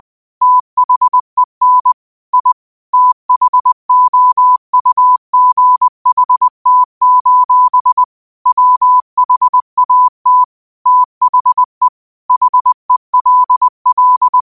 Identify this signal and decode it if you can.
Morse code, heard as a keyed tone, about 20 wpm, 1000 Hz. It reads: THEN I THOUGHT: WHAT THE HELL
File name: qotdfiles / 20wpm